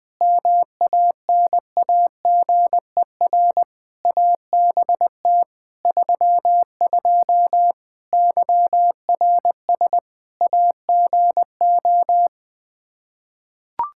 Morse code, heard as a keyed tone, about 20 wpm, 700 Hz. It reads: MANAGER ABT 32 YRS AGO
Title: Practice QSO - Exchange 3 - More Info - 20wpm